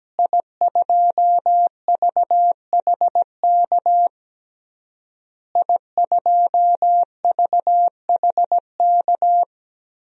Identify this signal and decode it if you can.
Morse code, heard as a keyed tone, about 17 wpm, 700 Hz. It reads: I2VHK I2VHK
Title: DXCC-100 Call Signs - 5 Characters - Repeated 3x - 17wpm